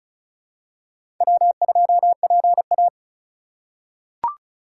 Morse code, heard as a keyed tone, about 36 wpm, 700 Hz. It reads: W2PA
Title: USA Call Signs - 4 Characters - 2x Character Spacing - 35wpm